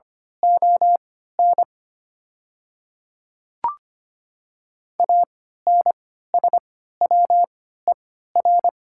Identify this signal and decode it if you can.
Morse code, heard as a keyed tone, about 25 wpm, spaced as for 14 wpm, 690 Hz. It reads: ON ANSWER
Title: 3x Character Spacing - Top 500 Words - 25wpm